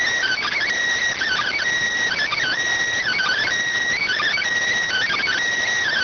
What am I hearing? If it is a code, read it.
Emergency Alert System (SAME) tones (header bursts), heard as harsh buzzing bursts. It is heard from 1.2 to 5.4 s.